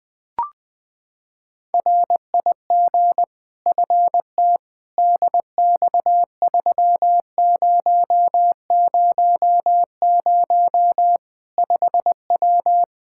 Morse code, heard as a keyed tone, about 20 wpm, 700 Hz. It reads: RIG FT DX3000 5W
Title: Practice QSO - Exchange 2 - Info - 20wpm